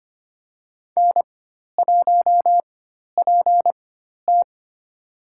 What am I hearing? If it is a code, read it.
Morse code, heard as a keyed tone, about 25 wpm, spaced as for 12 wpm, 700 Hz. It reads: N1PT
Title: USA Call Signs - 4 Characters - 4x Character Spacing - 25wpm